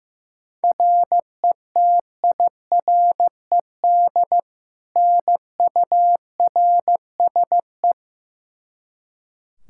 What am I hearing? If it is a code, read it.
Morse code, heard as a keyed tone, about 15 wpm, 700 Hz. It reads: RETIRED NURSE
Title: Practice QSO - Exchange 3 - More Info - 15wpm